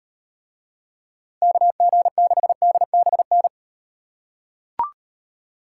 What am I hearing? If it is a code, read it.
Morse code, heard as a keyed tone, about 35 wpm, 700 Hz. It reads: KG6DBN